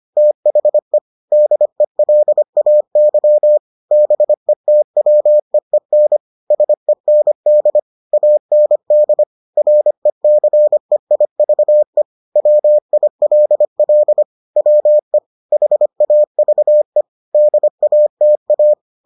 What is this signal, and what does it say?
Morse code, heard as a keyed tone, about 25 wpm, 600 Hz. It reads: THE DELAY BETWEEN SEND AND RECEIVE WILL WE HAVE DATA